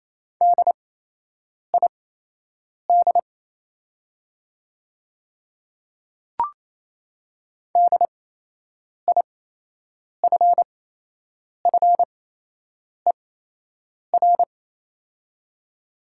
Morse code, heard as a keyed tone, about 28 wpm, spaced as for 8 wpm, 700 Hz. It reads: DID DIFFER
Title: 8x Character Spacing - Top 200 Words - 28wpm